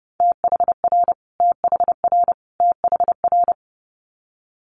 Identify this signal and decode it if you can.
Morse code, heard as a keyed tone, about 30 wpm, 700 Hz. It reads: THR THR THR
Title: Multi-letter Phonemes Repeated 3x 30wpm